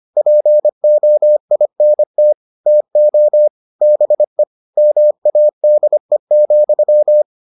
Morse code, heard as a keyed tone, about 25 wpm, 600 Hz. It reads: POINT TO BE MADE,